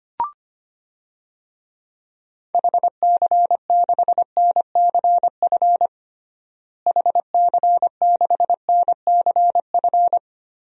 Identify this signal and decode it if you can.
Morse code, heard as a keyed tone, about 25 wpm, 700 Hz. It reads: HC6NCF HC6NCF
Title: DXCC-100 Call Signs - 6 Characters - Repeated 3x - 25wpm